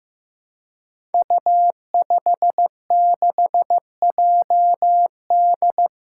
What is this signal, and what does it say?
Morse code, heard as a keyed tone, about 16 wpm, 700 Hz. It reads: U56JD
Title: Five Challenging Letter-Number Combinations 15wpm